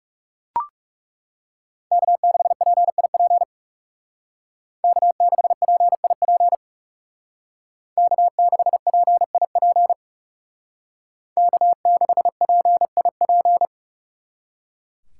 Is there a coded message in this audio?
Yes: Morse code from 1.9 to 13.7 s, heard as a keyed tone, about 39 wpm, 700 Hz. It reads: K6PIP K6PIP K6PIP K6PIP